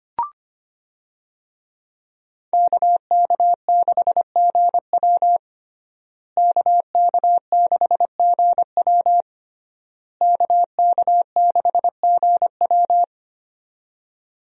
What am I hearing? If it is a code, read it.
Morse code, heard as a keyed tone, about 25 wpm, 700 Hz. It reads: KK6GW KK6GW KK6GW
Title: USA Call Signs - 5 Characters - Repeated 3x - 25wpm